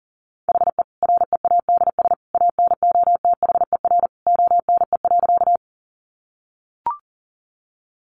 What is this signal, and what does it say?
Morse code, heard as a keyed tone, about 40 wpm, 700 Hz. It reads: HE READS ANOTHER ONE.
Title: Additional Easy Phrases 40wpm